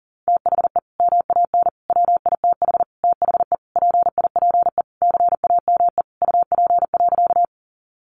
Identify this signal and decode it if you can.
Morse code, heard as a keyed tone, about 40 wpm, 700 Hz. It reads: THE MAN WITH THE PIPE CAME UP.
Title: Sentences from Top 1500 Words 40wpm